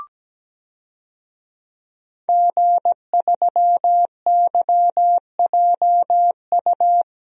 Morse code, heard as a keyed tone, about 17 wpm, 700 Hz. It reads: G3YJU
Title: DXCC-100 Call Signs - 5 Characters - Repeated 3x - 17wpm